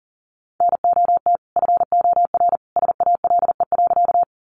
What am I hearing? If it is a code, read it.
Morse code, heard as a keyed tone, about 40 wpm, 700 Hz. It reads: NOT FOR SALE.
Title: Sentences from Top 1400 Words 40wpm